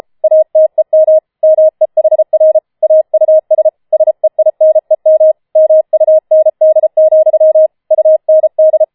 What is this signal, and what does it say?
Morse code, heard as a keyed tone, about 33 wpm, 630 Hz. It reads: ATEM MEHR AUS SEINEM MUND, UND
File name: Schneewittchen Teil5 34wpm